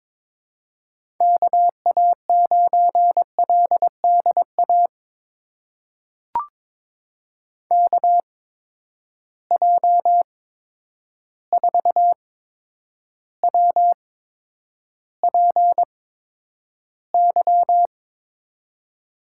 Morse code, heard as a keyed tone, about 22 wpm, spaced as for 9 wpm, 700 Hz. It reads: KA9LDA KJ4WPY
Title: USA Call Signs - 6 Characters - 8x Character Spacing - 22wpm